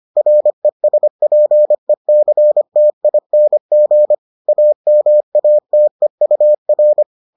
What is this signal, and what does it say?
Morse code, heard as a keyed tone, about 25 wpm, 600 Hz. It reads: RESPECTING AMATEUR